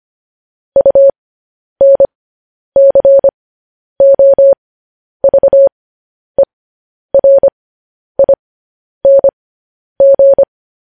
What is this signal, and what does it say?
Morse code, heard as a keyed tone, about 25 wpm, spaced as for 10 wpm, 550 Hz. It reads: UNCOVERING
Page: news headlines in morse code at 10 wpm